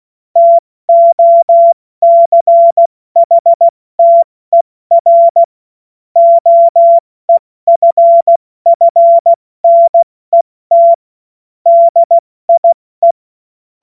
Morse code, heard as a keyed tone, about 16 wpm, 680 Hz. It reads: TOCHTER OEFFNET DIE
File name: Morsekurs / Witze / Witz 24. Januar 2023 70 F80